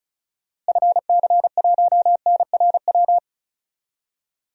Morse code, heard as a keyed tone, about 36 wpm, 700 Hz. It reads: FC1NRW
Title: DXCC-100 Call Signs - 6 Characters - Repeated 3x - 35wpm